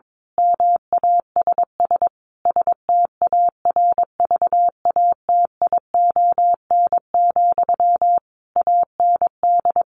Morse code, heard as a keyed tone, about 22 wpm, 700 Hz. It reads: MASS STARVATION, AND
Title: A Sound Of Thunder 22wpm